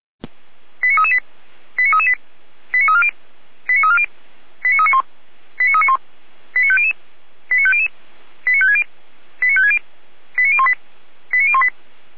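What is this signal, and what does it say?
Five-tone selective call (ZVEI), heard as quick runs of five tones, tones of about 70 ms. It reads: 89338 89338 89340 89340 89391 89391 89500 89500 89560 89560 89018 89018
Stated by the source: ZVEI five-tone call 89319 received by SDR at feuerwehr-orlamuende (Orlamünde fire brigade).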